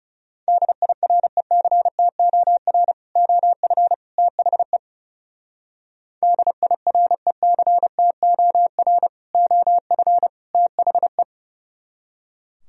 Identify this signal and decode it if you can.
Morse code, heard as a keyed tone, about 38 wpm, 700 Hz. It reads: DIRECTOR OF THE DIRECTOR OF THE